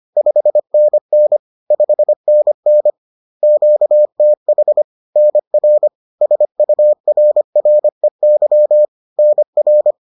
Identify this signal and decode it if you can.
Morse code, heard as a keyed tone, about 25 wpm, 600 Hz. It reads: 5NN 5NN QTH NR SURREY NR